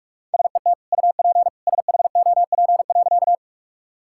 Morse code, heard as a keyed tone, about 46 wpm, 700 Hz. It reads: SET UP SHOP.
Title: Sentences from Top 1600 Words 45wpm